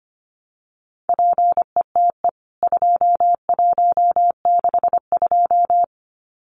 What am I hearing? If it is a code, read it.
Morse code, heard as a keyed tone, about 25 wpm, 700 Hz. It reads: PETE 2162
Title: CWT Exchange 25wpm